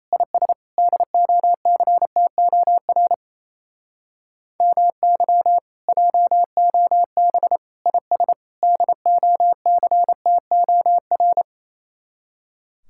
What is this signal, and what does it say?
Morse code, heard as a keyed tone, about 35 wpm, 700 Hz. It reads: IS DOCTOR MY JOB IS DOCTOR
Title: Speed-Racing Common QSO Phrases - 28wpm